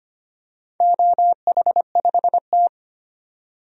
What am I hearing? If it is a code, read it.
Morse code, heard as a keyed tone, about 25 wpm, 690 Hz. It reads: OH5T
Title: DXCC-100 Call Signs - 4 Characters - 25wpm